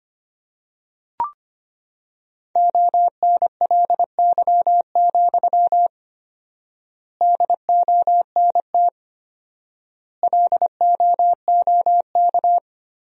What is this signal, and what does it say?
Morse code, heard as a keyed tone, about 25 wpm, 700 Hz. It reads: ONLY, DONT LOOK
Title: 4x Word-Spacing - Sentences from Top 300 Words - 25wpm